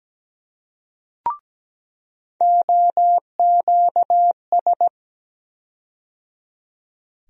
Morse code, heard as a keyed tone, about 17 wpm, 700 Hz. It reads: OQS
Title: USA Call Sign Postfixes - 3 Letters 17wpm